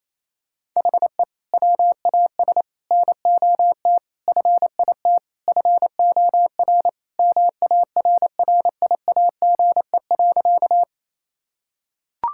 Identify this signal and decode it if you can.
Morse code, heard as a keyed tone, about 27 wpm, 700 Hz. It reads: HE WAS NOT FIT FOR MARRIAGE.
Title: Sentences from Top 1000 Words - Review Part 5 - 28wpm